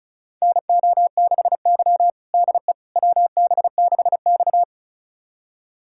Morse code, heard as a keyed tone, about 36 wpm, 700 Hz. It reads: NO6Y DE WB6X
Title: USA Callsign DE Callsign - 4 Characters - 35wpm